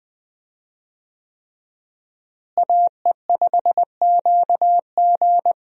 Morse code, heard as a keyed tone, about 20 wpm, 700 Hz. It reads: AE5QG
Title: USA Call Signs - 5 Characters - Repeated 3x - 20wpm